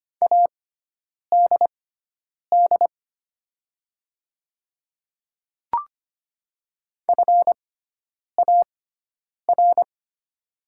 Morse code, heard as a keyed tone, about 25 wpm, spaced as for 9 wpm, 700 Hz. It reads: ADD FAR
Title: 6x Character Spacing - Top 300 Words - 25wpm